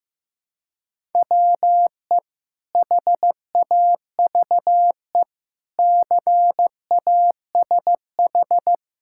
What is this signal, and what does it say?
Morse code, heard as a keyed tone, about 15 wpm, 700 Hz. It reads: WE HAVE CASH
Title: Sentences from Top 1000 Words - Review Part 5 - 15wpm